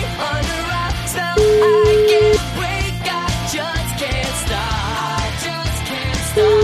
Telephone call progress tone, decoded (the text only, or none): ringback tone